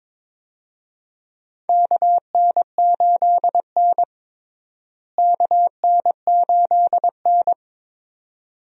Morse code, heard as a keyed tone, about 22 wpm, 700 Hz. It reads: KN8N KN8N
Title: USA Call Signs - 4 Characters - Repeated 2x - 22wpm